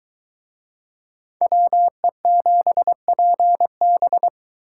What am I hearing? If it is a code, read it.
Morse code, heard as a keyed tone, about 23 wpm, 700 Hz. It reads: WE7PB